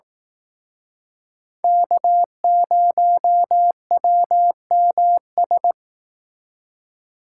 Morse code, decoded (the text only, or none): K0WMS